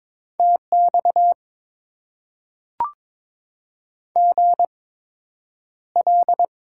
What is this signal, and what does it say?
Morse code, heard as a keyed tone, about 22 wpm, spaced as for 9 wpm, 700 Hz. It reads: TX GL
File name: Top 90 QSO Elements - 8x Character Spacing - 22wpm